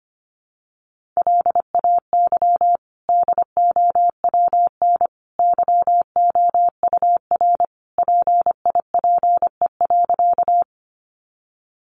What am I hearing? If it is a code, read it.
Morse code, heard as a keyed tone, about 25 wpm, 700 Hz. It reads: LAY DOWN YOUR PIPE.
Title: Sentences from Top 1500 Words 25wpm